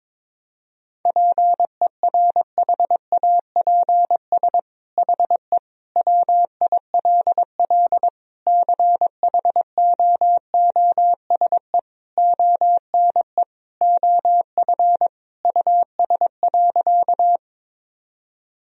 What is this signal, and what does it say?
Morse code, heard as a keyed tone, about 22 wpm, 700 Hz. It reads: PERHAPS HE WILL CHOOSE ONE OF US.
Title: Sentences from Top 1000 Words - Review Part 4 - 22wpm